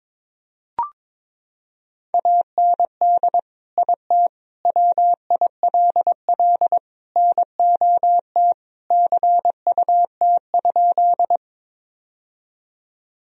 Morse code, heard as a keyed tone, about 22 wpm, 700 Hz. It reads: AND IT WILL NOT CUT?
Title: Sentences from Top 1000 Words - Review Part 1 - 22wpm